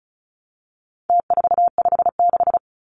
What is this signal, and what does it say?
Morse code, heard as a keyed tone, about 34 wpm, 700 Hz. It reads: T456
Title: Numbers - Contest Style - 4-Digit 35wpm